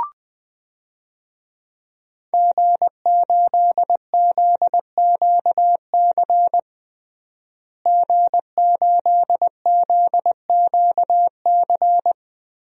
Morse code, heard as a keyed tone, about 21 wpm, 700 Hz. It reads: G8ZQC G8ZQC
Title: DXCC-100 Call Signs - 5 Characters - Repeated 3x - 20wpm